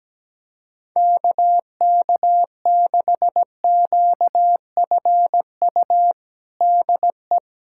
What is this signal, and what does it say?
Morse code, heard as a keyed tone, about 17 wpm, 700 Hz. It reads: KK6QFU DE
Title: USA Callsign DE Callsign - 6 Characters - 17wpm